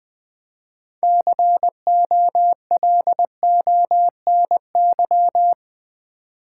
Morse code, heard as a keyed tone, about 20 wpm, 700 Hz. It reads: COLONY
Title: Build A Word - 2 Letters At A Time - 6-Letter Words - 20wpm